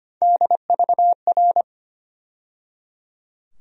Morse code, heard as a keyed tone, about 25 wpm, 700 Hz. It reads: DVR
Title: USA Call Sign Postfixes - 3 Letters 25wpm